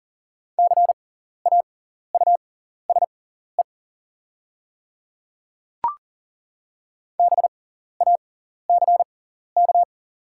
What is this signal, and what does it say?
Morse code, heard as a keyed tone, about 40 wpm, spaced as for 14 wpm, 700 Hz. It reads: CAUSE BACK
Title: 6x Character Spacing - Top 200 Words - 40wpm